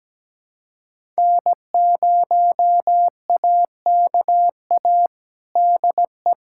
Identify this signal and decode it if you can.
Morse code, heard as a keyed tone, about 17 wpm, 700 Hz. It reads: N0AKA DE
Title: USA Callsign DE Callsign - 5 Characters - 17wpm